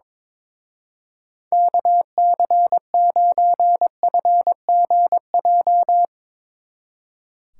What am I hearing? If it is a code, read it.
Morse code, heard as a keyed tone, about 22 wpm, 700 Hz. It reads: KC9FGJ
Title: Speed Racing - USA Call Signs 22wpm